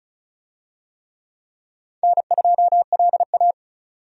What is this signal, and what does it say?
Morse code, heard as a keyed tone, about 36 wpm, 700 Hz. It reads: N2LA
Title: USA Call Signs - 4 Characters - 35wpm